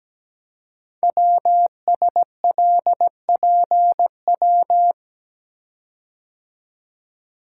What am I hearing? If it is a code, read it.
Morse code, heard as a keyed tone, about 17 wpm, 700 Hz. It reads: WSLPW
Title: Code Groups - 5 Letters - 17wpm